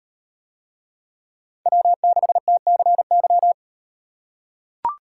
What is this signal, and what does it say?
Morse code, heard as a keyed tone, about 36 wpm, 700 Hz. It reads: W6TCY